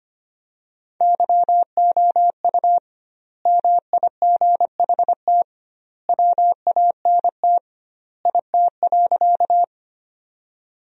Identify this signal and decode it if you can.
Morse code, heard as a keyed tone, about 25 wpm, 700 Hz. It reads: YOU MIGHT WANT IT.
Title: 2x Word-Spacing - Sentences from Top 300 Words - 25wpm